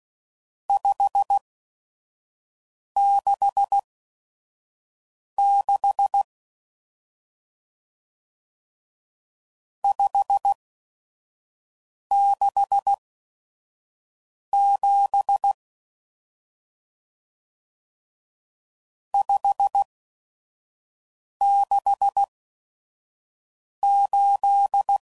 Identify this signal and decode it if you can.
Morse code, heard as a keyed tone, about 16 wpm, spaced as for 5 wpm, 800 Hz. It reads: 566 567 568
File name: CW Code Groups @5WPM / RST's / Various RST No's